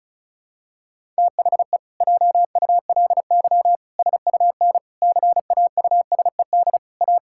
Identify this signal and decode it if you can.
Morse code, heard as a keyed tone, about 36 wpm, 700 Hz. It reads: THE JULY SUN CAUSED A
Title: Pangram Challenge v2 - 35wpm